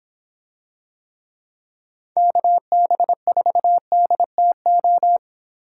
Morse code, heard as a keyed tone, about 26 wpm, 700 Hz. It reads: KB4DTO